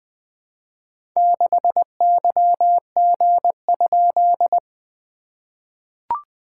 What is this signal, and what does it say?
Morse code, heard as a keyed tone, about 20 wpm, 700 Hz. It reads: 6YG?
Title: SOTA QSO Practice 20wpm